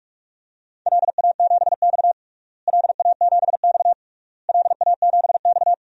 Morse code, heard as a keyed tone, about 46 wpm, 700 Hz. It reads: LA7X LA7X LA7X
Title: DXCC-100 Call Signs - 4 Characters - Repeated 3x - 45wpm